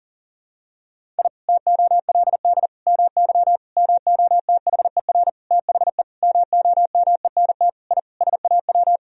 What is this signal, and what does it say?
Morse code, heard as a keyed tone, about 38 wpm, 700 Hz. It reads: I TOLD MY MOTHER THE MOMENT I SAW
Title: Sentences from Top 1000 Words - Review Part 1 - 40wpm